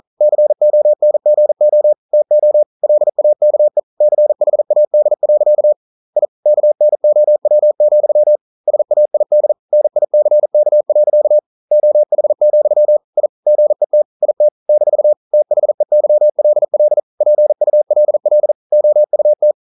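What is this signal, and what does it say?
Morse code, heard as a keyed tone, about 42 wpm, 600 Hz. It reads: CONGO TO LAKE CHAD. I KNOW, SAID NICK. OH, I GET IT - THEYLL PULL OUT
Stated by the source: Morse code practice at 41 wpm, from CW med Gnister.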